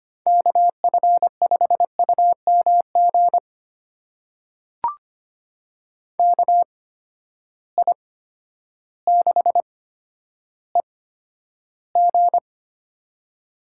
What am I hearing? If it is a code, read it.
Morse code, heard as a keyed tone, about 25 wpm, spaced as for 11 wpm, 700 Hz. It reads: KF5UMG KI6EG
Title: USA Call Signs - 8x Character Spacing - 25wpm